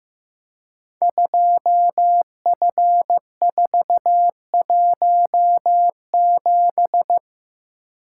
Morse code, heard as a keyed tone, about 15 wpm, 700 Hz. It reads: 2F417
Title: Code Groups - 5 Letter:Numbers - 15wpm